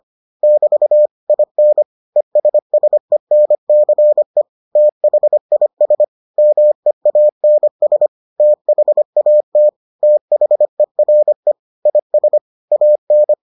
Morse code, heard as a keyed tone, about 25 wpm, 600 Hz. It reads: <BT> IN ESSENCE THIS MEANS THAT THERE IS AN